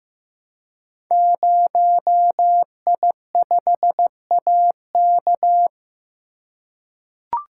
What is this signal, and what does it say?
Morse code, heard as a keyed tone, about 15 wpm, 700 Hz. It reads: 0I5AK
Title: Code Groups - 5 Letter:Numbers - 15wpm